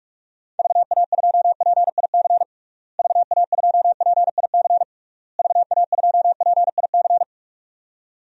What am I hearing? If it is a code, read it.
Morse code, heard as a keyed tone, about 46 wpm, 700 Hz. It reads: VA2PIC VA2PIC VA2PIC
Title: DXCC-100 Call Signs - 6 Characters - Repeated 3x - 45wpm